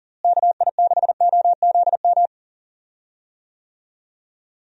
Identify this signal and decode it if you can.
Morse code, heard as a keyed tone, about 37 wpm, 700 Hz. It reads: KI6OZM
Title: USA Call Signs - 6 Characters - Repeated 3x - 40wpm